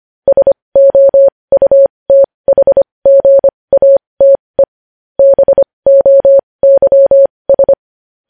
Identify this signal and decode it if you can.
Morse code, heard as a keyed tone, about 25 wpm, spaced as for 20 wpm, 550 Hz. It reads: SOUTHGATE BOYS
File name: podcast / News headlines-1x-25x20